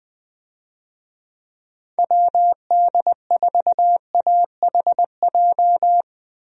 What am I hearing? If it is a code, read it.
Morse code, heard as a keyed tone, about 20 wpm, 700 Hz. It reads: WD4AHJ
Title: USA Call Signs - 6 Characters - Repeated 2x - 20wpm